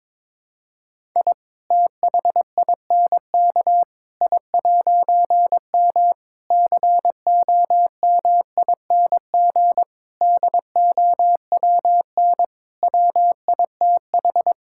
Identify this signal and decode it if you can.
Morse code, heard as a keyed tone, about 23 wpm, 700 Hz. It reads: I THINK I'M COMING DOWN WITH
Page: morse-code-ninja-practice-22wpm